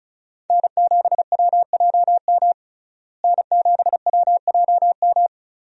Morse code, heard as a keyed tone, about 34 wpm, 700 Hz. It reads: N7WJM N7WJM
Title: USA Call Signs - 5 Characters - Repeated 3x - 35wpm